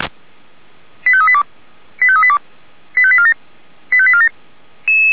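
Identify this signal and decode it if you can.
Five-tone selective call (ZVEI), heard as quick runs of five tones, tones of about 70 ms. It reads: 85382 85382 85847 85847
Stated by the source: Five-tone ZVEI selective call received on an SDR.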